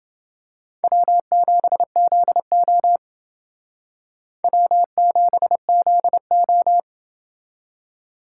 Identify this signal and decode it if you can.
Morse code, heard as a keyed tone, about 29 wpm, 700 Hz. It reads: W7ZO W7ZO